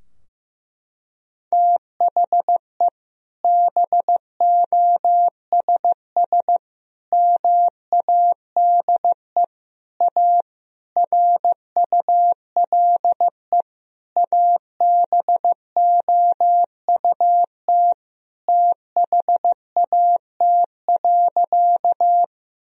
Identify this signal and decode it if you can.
Morse code, heard as a keyed tone, about 16 wpm, 700 Hz. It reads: THE BOSS MADE A RULE ABOUT THAT.
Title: Sentences from Top 1000 Words - Review Part 4 - 15wpm